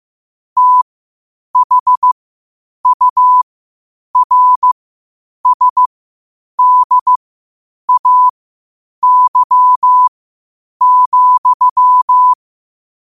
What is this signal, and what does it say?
Morse code, heard as a keyed tone, about 15 wpm, spaced as for 8 wpm, 1000 Hz. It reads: THURSDAY,
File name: qotdfiles / 8wpm